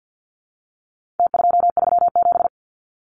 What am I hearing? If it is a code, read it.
Morse code, heard as a keyed tone, about 49 wpm, 700 Hz. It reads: T237
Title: Numbers - Contest Style - 4-Digit 50wpm